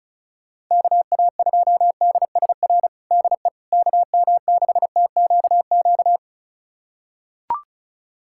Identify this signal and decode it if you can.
Morse code, heard as a keyed tone, about 36 wpm, 700 Hz. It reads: KA2DSR DE KM6TQQ
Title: USA Callsign DE Callsign 35wpm